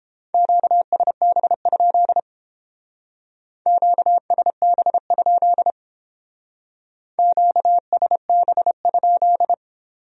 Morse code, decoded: QSB? QSB? QSB?

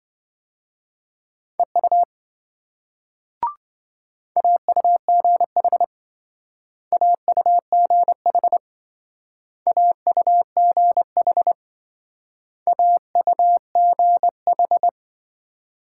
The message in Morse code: EU AUGH AUGH AUGH AUGH

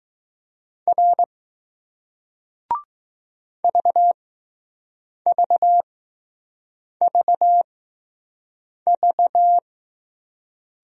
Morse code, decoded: R VVVV

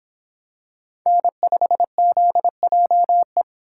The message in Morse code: N5ZJE

